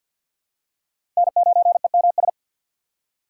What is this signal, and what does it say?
Morse code, heard as a keyed tone, about 51 wpm, 700 Hz. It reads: N9EMS
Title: USA Call Signs - 5 Characters - 4x Character Spacing - 50wpm